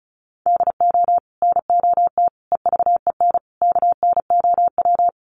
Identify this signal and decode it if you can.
Morse code, heard as a keyed tone, about 34 wpm, 700 Hz. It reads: DO NOT EVEN KNOW
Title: Sets of 4 Words 35wpm